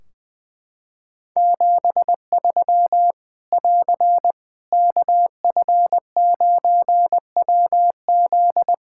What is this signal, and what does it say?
Morse code, heard as a keyed tone, about 20 wpm, 700 Hz. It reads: 73 <AR> KF9WZ
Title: K1USN SST QSO Practice 20wpm